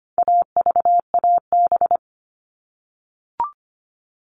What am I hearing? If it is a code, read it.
Morse code, heard as a keyed tone, about 25 wpm, 700 Hz. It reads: AVAB
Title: All Cut Numbers - 4-Digit 25wpm